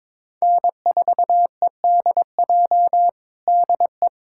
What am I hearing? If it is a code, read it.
Morse code, heard as a keyed tone, about 21 wpm, 700 Hz. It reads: N4EDJ DE
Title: USA Callsign DE Callsign - 5 Characters - 22wpm